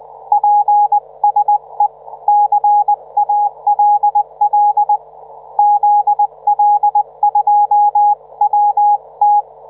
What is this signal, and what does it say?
Morse code, heard as a keyed tone, about 20 wpm, 830 Hz, with radio noise. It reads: PSE CALL ZL2WT